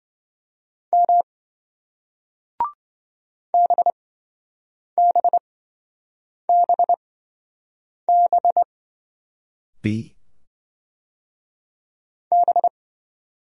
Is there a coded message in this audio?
Yes: Morse code from 0.9 to 12.7 s, heard as a keyed tone, about 27 wpm, spaced as for 7 wpm, 700 Hz. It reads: M BBBB B